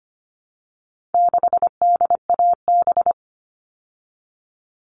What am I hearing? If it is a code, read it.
Morse code, heard as a keyed tone, about 25 wpm, 700 Hz. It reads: 6DAB
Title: All Cut Numbers - 4-Digit 25wpm